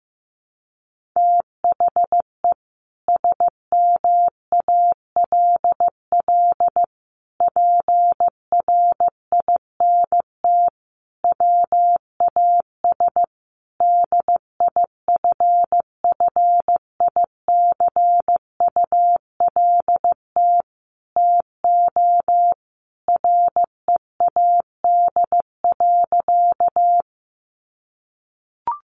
Morse code, heard as a keyed tone, about 15 wpm, 700 Hz. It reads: THE SMALL PRINT WAS DIFFICULT TO READ.
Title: Sentences from Top 1400 Words 15wpm